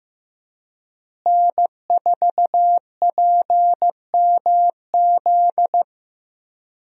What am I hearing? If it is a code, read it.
Morse code, heard as a keyed tone, about 15 wpm, 700 Hz. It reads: N4PMZ
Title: USA Call Signs - 5 Characters - Repeated 3x - 15wpm